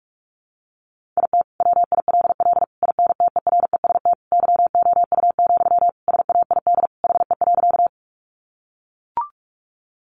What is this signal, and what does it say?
Morse code, heard as a keyed tone, about 44 wpm, 700 Hz. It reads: IT WILL INTEREST YOU, SAID HE.
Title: Sentences from Top 400 Words 45wpm